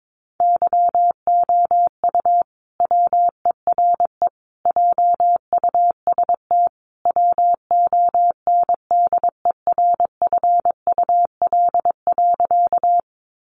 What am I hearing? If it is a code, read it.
Morse code, heard as a keyed tone, about 22 wpm, 700 Hz. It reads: YOU WERE JUST WONDERFUL.
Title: Sentences from Top 1100 Words 22wpm